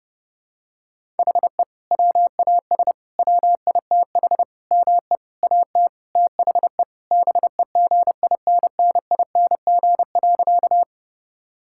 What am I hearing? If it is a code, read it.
Morse code, heard as a keyed tone, about 30 wpm, 700 Hz. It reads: HE WAS WITH ME AT THE BEGINNING.
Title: Sentences from Top 1000 Words - Review Part 5 - 30wpm